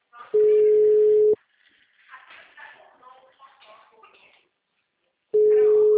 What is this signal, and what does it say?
Ringback tone (Europe, 425 Hz), with speech.